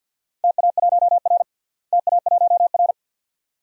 Morse code, heard as a keyed tone, about 51 wpm, 700 Hz. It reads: TA1R TA1R
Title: DXCC-100 Call Signs - 4 Characters - Repeated 2x - 50wpm